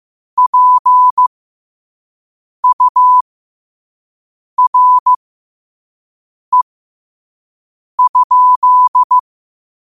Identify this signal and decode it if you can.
Morse code, heard as a keyed tone, about 15 wpm, spaced as for 5 wpm, 1000 Hz. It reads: PURE?
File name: qotdfiles / 5wpm